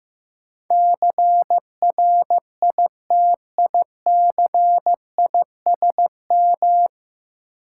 Morse code, heard as a keyed tone, about 15 wpm, 700 Hz. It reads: CRITICISM
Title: Sentences from Top 1300 Words - New Word Drill Alt - 15wpm